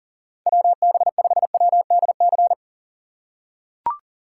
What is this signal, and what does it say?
Morse code, heard as a keyed tone, about 40 wpm, 700 Hz. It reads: WB5WDC
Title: USA Call Signs - 6 Characters - 4x Character Spacing - 40wpm